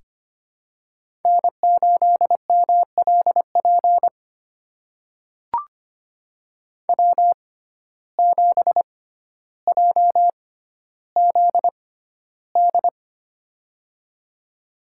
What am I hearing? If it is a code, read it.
Morse code, heard as a keyed tone, about 25 wpm, spaced as for 13 wpm, 700 Hz. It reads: N8MLP W7JZD
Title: USA Call Signs - 5 Characters - 6x Character Spacing - 25wpm